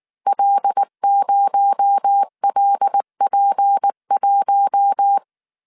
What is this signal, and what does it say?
Morse code, heard as a keyed tone, about 19 wpm, 800 Hz. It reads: L0LP1
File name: CW / DET / 1 6 1500C S C 18wpm 2356